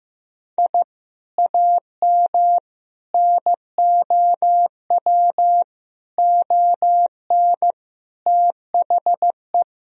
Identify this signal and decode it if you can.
Morse code, heard as a keyed tone, about 15 wpm, 700 Hz. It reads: I AM NOW ON THE
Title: Sentences from Top 1000 Words - Review Part 2 - 15wpm